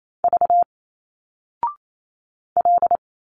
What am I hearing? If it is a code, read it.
Morse code, heard as a keyed tone, about 28 wpm, 700 Hz. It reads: V L
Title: Single Letters 28wpm